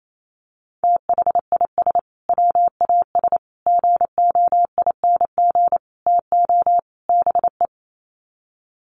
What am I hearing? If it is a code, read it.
Morse code, heard as a keyed tone, about 28 wpm, 700 Hz. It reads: THIS WAS GOING TO BE
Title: Sets of 5 Words Encore2 28wpm